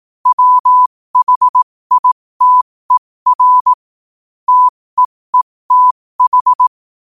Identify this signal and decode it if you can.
Morse code, heard as a keyed tone, about 18 wpm, spaced as for 15 wpm, 1000 Hz. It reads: WHITER TEETH
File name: qotdfiles / 15wpm